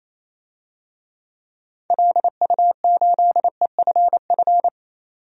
Morse code, heard as a keyed tone, about 27 wpm, 700 Hz. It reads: LU8EFF
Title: DXCC-100 Call Signs - 6 Characters - 28wpm